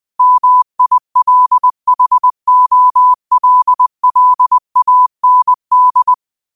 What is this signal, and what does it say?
Morse code, heard as a keyed tone, about 20 wpm, 1000 Hz. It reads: MILHOLLAND
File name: qotdfiles / 20wpm